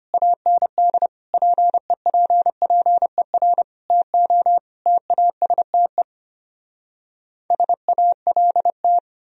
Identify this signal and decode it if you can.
Morse code, heard as a keyed tone, about 33 wpm, 700 Hz. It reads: AND PEPPER TO TASTE SALT